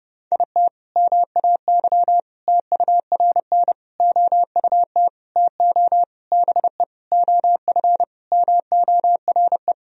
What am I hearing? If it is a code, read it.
Morse code, heard as a keyed tone, about 30 wpm, 700 Hz. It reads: IT MAY TURN OUT TO BE OF MORE
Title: Sentences from Top 400 Words and Abbreviations 30wpm